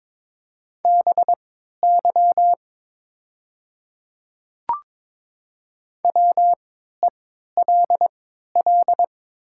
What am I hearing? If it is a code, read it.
Morse code, heard as a keyed tone, about 22 wpm, spaced as for 13 wpm, 700 Hz. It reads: BY WELL
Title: 3x Character Spacing - Top 300 Words - 22wpm